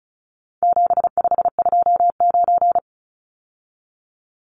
Morse code, heard as a keyed tone, about 34 wpm, 700 Hz. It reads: 7529
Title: Numbers - 4-Digit 35wpm